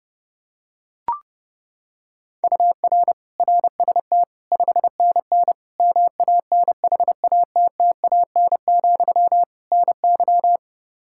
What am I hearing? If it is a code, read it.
Morse code, heard as a keyed tone, about 30 wpm, 700 Hz. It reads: UR RST 5NN MANHATTAN, NY
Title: Practice QSO - Exchange 1 - Beginning - 30wpm